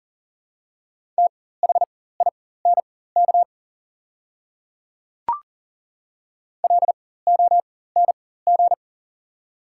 Morse code, heard as a keyed tone, about 40 wpm, spaced as for 19 wpm, 700 Hz. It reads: THINK LONG
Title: 4x Character Spacing - Top 500 Words - 40wpm